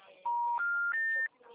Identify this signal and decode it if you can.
Special information tone.